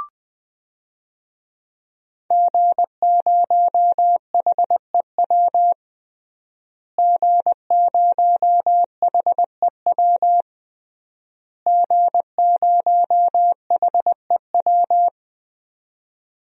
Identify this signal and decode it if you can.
Morse code, heard as a keyed tone, about 20 wpm, 700 Hz. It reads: G0HEW G0HEW G0HEW
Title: DXCC-100 Call Signs - 5 Characters - Repeated 3x - 20wpm